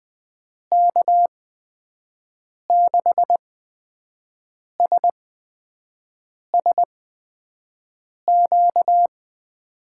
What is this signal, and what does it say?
Morse code, heard as a keyed tone, about 20 wpm, spaced as for 5 wpm, 700 Hz. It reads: K6SSQ
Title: USA Call Signs - 6 Characters - 8x Character Spacing - 20wpm